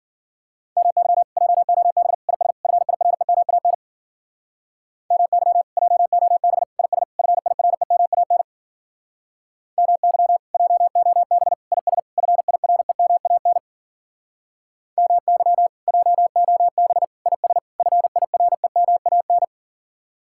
Morse code, heard as a keyed tone, about 53 wpm, 700 Hz. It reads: MY JOB IS FIREMAN MY JOB IS FIREMAN MY JOB IS FIREMAN MY JOB IS FIREMAN